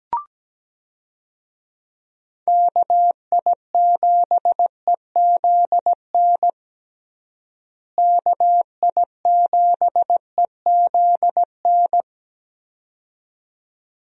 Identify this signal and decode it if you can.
Morse code, heard as a keyed tone, about 17 wpm, 700 Hz. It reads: KI7EZN KI7EZN
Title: USA Call Signs - 6 Characters - Repeated 2x - 17wpm